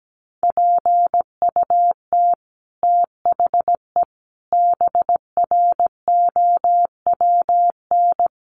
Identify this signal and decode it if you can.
Morse code, heard as a keyed tone, about 17 wpm, 700 Hz. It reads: PUT THE BROWN
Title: Sentences from Top 1200 Words 17wpm